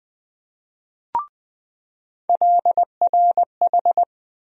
Morse code, heard as a keyed tone, about 20 wpm, 700 Hz. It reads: LRH
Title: USA Call Sign Postfixes - 3 Letters 20wpm